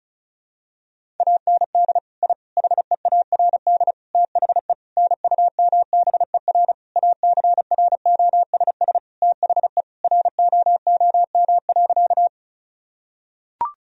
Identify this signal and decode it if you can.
Morse code, heard as a keyed tone, about 34 wpm, 690 Hz. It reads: AND I HEARD THE NUMBER ACROSS THE ROOM.
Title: Sentences from Top 1000 Words - Review Part 1 - 35wpm